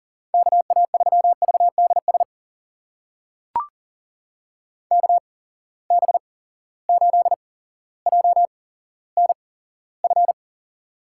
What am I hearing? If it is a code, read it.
Morse code, heard as a keyed tone, about 40 wpm, spaced as for 17 wpm, 700 Hz. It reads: KA3VDS KB8JNF
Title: USA Call Signs - 6 Characters - 8x Character Spacing - 40wpm